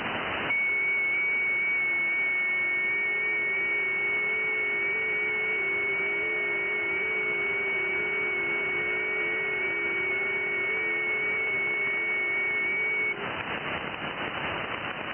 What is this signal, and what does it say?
FT8.
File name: IC-7100